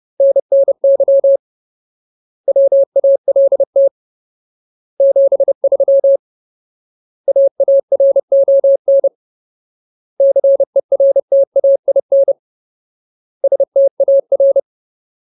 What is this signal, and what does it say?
Morse code, heard as a keyed tone, about 30 wpm, 550 Hz. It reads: NNY WALT 73 AARON CERTAIN STAR
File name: Ultimate Word List / 30WPM Ultimate